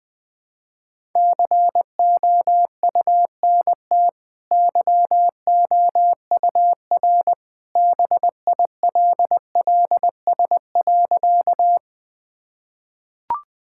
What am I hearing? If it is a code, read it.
Morse code, heard as a keyed tone, about 20 wpm, 700 Hz. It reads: COUNT YOUR BILLS.
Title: Easy Sentences with Hints - 20wpm